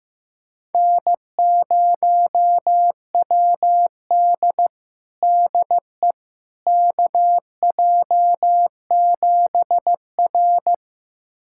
Morse code, heard as a keyed tone, about 15 wpm, 700 Hz. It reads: N0WD DE KJ7R
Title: USA Callsign DE Callsign - 4 Characters - 15wpm